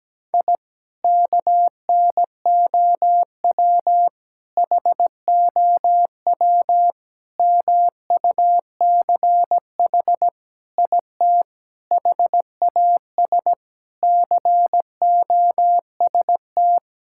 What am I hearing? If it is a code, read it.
Morse code, heard as a keyed tone, about 17 wpm, 700 Hz. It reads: I KNOW HOW MUCH IT HAS COST
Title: Sentences from Top 1000 Words - Review Part 5 - 17wpm